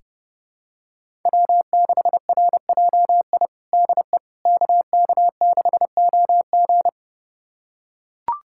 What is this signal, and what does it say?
Morse code, heard as a keyed tone, about 30 wpm, 700 Hz. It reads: W6RJI DE KK6OG
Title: USA Callsign DE Callsign - 5 Characters - 30wpm